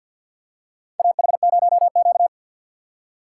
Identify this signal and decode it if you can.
Morse code, heard as a keyed tone, about 52 wpm, 700 Hz. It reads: AH0Q
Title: USA Call Signs - 4 Characters - 6x Character Spacing - 50wpm